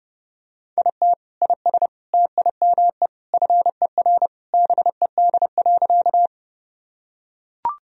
Morse code, heard as a keyed tone, about 30 wpm, 700 Hz. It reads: IT IS TIME FER BED.
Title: Sentences from Top 500 Words and Abbreviations 30wpm